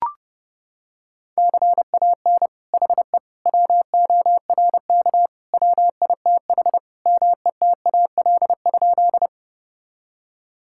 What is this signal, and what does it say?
Morse code, heard as a keyed tone, about 29 wpm, 700 Hz. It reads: CAN HE WORK WITH METAL?
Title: Sentences from Top 1000 Words - Review Part 6 - 30wpm